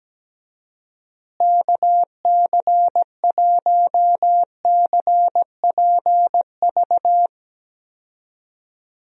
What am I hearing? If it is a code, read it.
Morse code, heard as a keyed tone, about 17 wpm, 700 Hz. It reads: KC1CPV
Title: USA Call Signs - 6 Characters - Repeated 3x - 17wpm